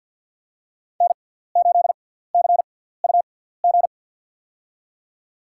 Morse code, heard as a keyed tone, about 52 wpm, spaced as for 17 wpm, 700 Hz. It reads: N8CUG
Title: USA Call Signs - 5 Characters - 6x Character Spacing - 50wpm